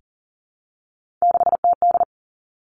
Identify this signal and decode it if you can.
Morse code, heard as a keyed tone, about 42 wpm, 700 Hz. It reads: 6TD
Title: Three Challenging Letter-Number Combinations 40wpm